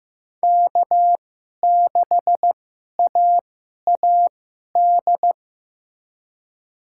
Morse code, heard as a keyed tone, about 15 wpm, spaced as for 11 wpm, 700 Hz. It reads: K6AAD
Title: USA Call Signs - 5 Characters - 2x Character Spacing - 15wpm